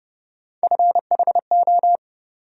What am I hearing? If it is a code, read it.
Morse code, heard as a keyed tone, about 30 wpm, 700 Hz. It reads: FHO
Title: USA Call Sign Postfixes - 3 Letters 30wpm